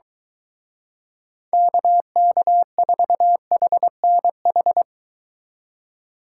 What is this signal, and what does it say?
Morse code, heard as a keyed tone, about 23 wpm, 700 Hz. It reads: KK4HNH